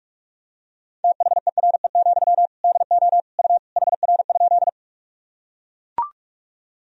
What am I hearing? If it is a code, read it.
Morse code, heard as a keyed tone, about 42 wpm, 700 Hz. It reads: THERE, DO U HR?
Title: Sentences from Top 300 Words and Abbreviations 45wpm